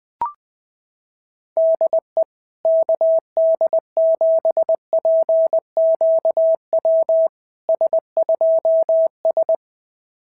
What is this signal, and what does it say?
Morse code, heard as a keyed tone, about 21 wpm, 660 Hz. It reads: DE KD7PQW S2S
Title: SOTA QSO Practice 20wpm